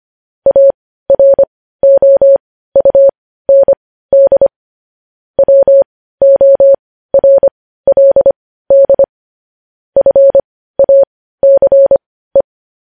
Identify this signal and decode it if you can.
Morse code, heard as a keyed tone, about 25 wpm, spaced as for 15 wpm, 550 Hz. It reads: AROUND WORLD FACE
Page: news headlines in morse code at 15 wpm